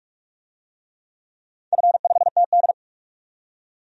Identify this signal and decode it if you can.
Morse code, heard as a keyed tone, about 46 wpm, 700 Hz. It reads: F5TD
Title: DXCC-100 Call Signs - 4 Characters - 45wpm